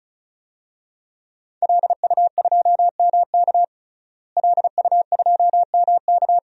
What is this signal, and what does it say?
Morse code, heard as a keyed tone, about 36 wpm, 700 Hz. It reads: LU2MK LU2MK
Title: DXCC-100 Call Signs - 5 Characters - Repeated 3x - 35wpm